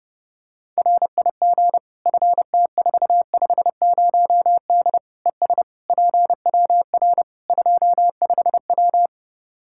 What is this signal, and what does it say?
Morse code, heard as a keyed tone, about 30 wpm, 700 Hz. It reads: RIG FT450D ES PWR 25W
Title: K1USN SST QSO Practice 30wpm